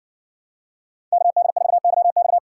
Morse code, heard as a keyed tone, about 57 wpm, 700 Hz. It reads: KD4YX